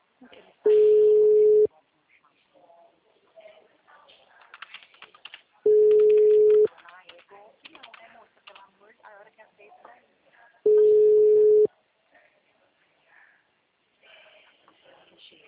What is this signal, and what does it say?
Ringback tone (Europe, 425 Hz), with speech.